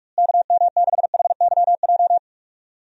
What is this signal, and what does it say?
Morse code, heard as a keyed tone, about 42 wpm, 700 Hz. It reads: KM6HYJ
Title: USA Call Signs - 6 Characters - Repeated 3x - 45wpm